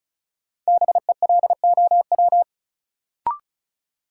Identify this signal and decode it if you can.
Morse code, heard as a keyed tone, about 34 wpm, 700 Hz. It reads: BELOW
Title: Sentences from Top 1000 Words - Review Part 5 - New Word Drill - 35wpm